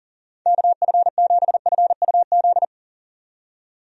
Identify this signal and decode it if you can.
Morse code, heard as a keyed tone, about 40 wpm, 700 Hz. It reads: KF7FUZ
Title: USA Call Signs - 6 Characters - 40wpm